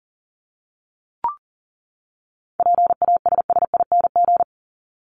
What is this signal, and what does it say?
Morse code, heard as a keyed tone, about 39 wpm, 700 Hz. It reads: PASSING
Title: ING Suffix - Top 100 Words with Base Words 40wpm